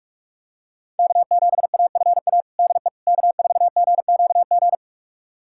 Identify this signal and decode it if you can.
Morse code, heard as a keyed tone, about 46 wpm, 700 Hz. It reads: K7AFA DE K4GQG
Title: USA Callsign DE Callsign - 5 Characters - 45wpm